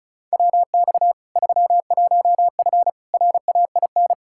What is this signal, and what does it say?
Morse code, heard as a keyed tone, about 36 wpm, 700 Hz. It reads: WX 31F RAIN
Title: Practice QSO - Exchange 2 - Info - 35wpm